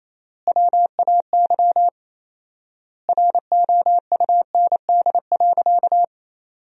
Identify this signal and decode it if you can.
Morse code, heard as a keyed tone, about 28 wpm, 700 Hz. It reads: WAY ROUND.
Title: 4x Word-Spacing - Sentences from Top 200 Words - 28wpm